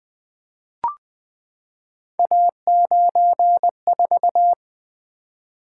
Morse code, heard as a keyed tone, about 20 wpm, 700 Hz. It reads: A94
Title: Three Letter-Number Combinations - 20wpm